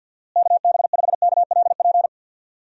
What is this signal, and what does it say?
Morse code, heard as a keyed tone, about 51 wpm, 700 Hz. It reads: KB5KLP
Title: USA Call Signs - 6 Characters - Repeated 2x - 50wpm